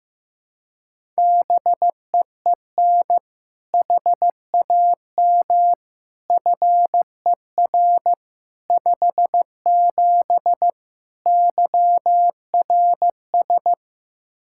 Morse code, heard as a keyed tone, about 15 wpm, 700 Hz. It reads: BEEN HAM FER 57 YRS
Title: K1USN SST QSO Practice 15wpm